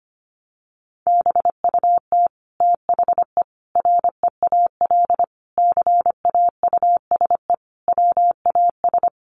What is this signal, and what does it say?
Morse code, heard as a keyed tone, about 25 wpm, 700 Hz. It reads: BUT THE REAL CAUSE WAS
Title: Sentences from Top 1200 Words 25wpm